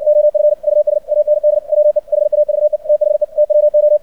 Morse code, heard as a keyed tone, about 59 wpm, 600 Hz. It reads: OF US AND WE HAVE THE TWO
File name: archived / CRSnet